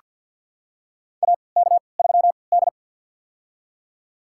Morse code, heard as a keyed tone, about 47 wpm, spaced as for 28 wpm, 700 Hz. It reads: AK3D
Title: USA Call Signs - 3x Character Spacing - 50wpm